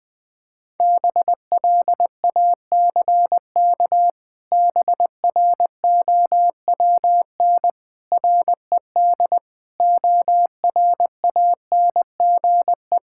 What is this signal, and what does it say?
Morse code, heard as a keyed tone, about 19 wpm, 700 Hz. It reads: BLACK BROWN RED ORANGE
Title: Pangram Challenge v2 - 20wpm